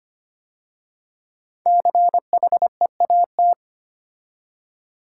Morse code, heard as a keyed tone, about 25 wpm, 700 Hz. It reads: CHEAT
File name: 5 Letter Words - First 3 Letters Given - 25wpm